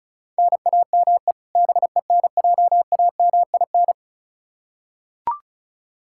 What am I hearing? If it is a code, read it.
Morse code, heard as a keyed tone, about 36 wpm, 700 Hz. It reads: NAME BENJAMIN
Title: K1USN SST QSO Practice 35wpm